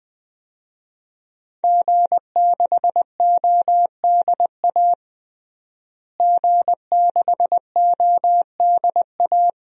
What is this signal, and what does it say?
Morse code, heard as a keyed tone, about 20 wpm, 700 Hz. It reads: G6ODA G6ODA
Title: DXCC-100 Call Signs - 5 Characters - Repeated 2x - 20wpm